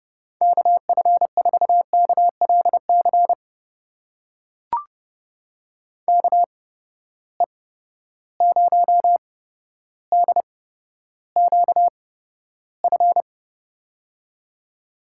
Morse code, heard as a keyed tone, about 32 wpm, spaced as for 13 wpm, 700 Hz. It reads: KF4KLC KE0DQF
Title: USA Call Signs - 6 Characters - 8x Character Spacing - 30wpm